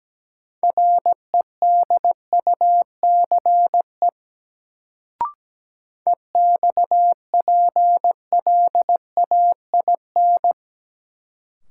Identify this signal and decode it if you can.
Morse code, heard as a keyed tone, about 17 wpm, 700 Hz. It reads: REDUCE EXPLAIN
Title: Sentences from Top 1100 Words - New Word Drill Alt - 17wpm